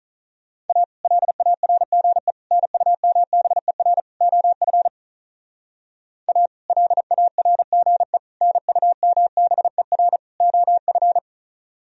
Morse code, heard as a keyed tone, about 44 wpm, 700 Hz. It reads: A LARGE NUMBER OF A LARGE NUMBER OF